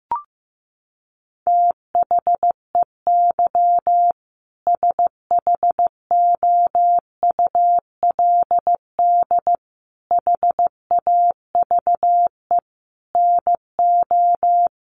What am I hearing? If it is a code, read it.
Morse code, heard as a keyed tone, about 15 wpm, 700 Hz. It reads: THEY SHOULD HAVE NO
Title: Sentences from Top 1500 Words 15wpm